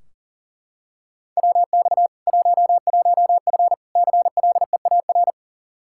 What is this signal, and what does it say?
Morse code, heard as a keyed tone, about 37 wpm, 700 Hz. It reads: WX 11F CLEAR
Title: Practice QSO - Exchange 2 - Info - 40wpm